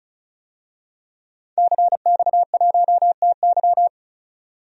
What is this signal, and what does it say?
Morse code, heard as a keyed tone, about 36 wpm, 700 Hz. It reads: CX1TY
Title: DXCC-100 Call Signs - 5 Characters - 35wpm